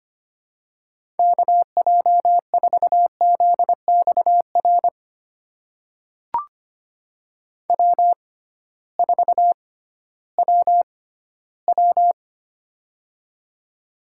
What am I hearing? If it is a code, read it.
Morse code, heard as a keyed tone, about 25 wpm, spaced as for 14 wpm, 700 Hz. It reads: KJ4ZXR W4WW
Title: USA Call Signs - 6x Character Spacing - 25wpm